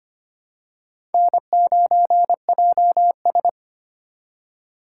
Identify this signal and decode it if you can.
Morse code, heard as a keyed tone, about 25 wpm, 700 Hz. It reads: N9JS
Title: USA Call Signs - 4 Characters - 2x Character Spacing - 25wpm